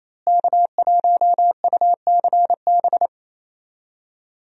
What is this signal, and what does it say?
Morse code, heard as a keyed tone, about 27 wpm, 700 Hz. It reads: K1UCB
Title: USA Call Signs - 5 Characters - Repeated 2x - 28wpm